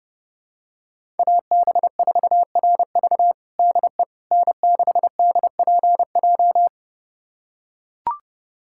Morse code, heard as a keyed tone, about 30 wpm, 700 Hz. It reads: AB4RV DE N6DPJ
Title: USA Callsign DE Callsign - 5 Characters - 30wpm